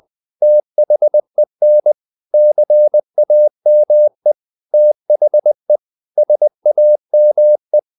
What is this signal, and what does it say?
Morse code, heard as a keyed tone, about 20 wpm, 600 Hz. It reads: THEN CAME THE SAME
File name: the old man and the sea / the old man and the sea part 1 20wpm